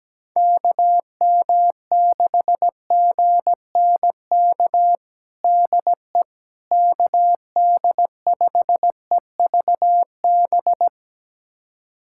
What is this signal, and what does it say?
Morse code, heard as a keyed tone, about 17 wpm, 700 Hz. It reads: KM6GNK DE KD5EVB
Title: USA Callsign DE Callsign - 6 Characters - 17wpm